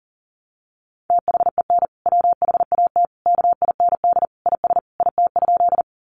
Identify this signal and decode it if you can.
Morse code, heard as a keyed tone, about 40 wpm, 700 Hz. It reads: THEN WHAT KIND IS IT?
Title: Sentences from Top 200 Words 40wpm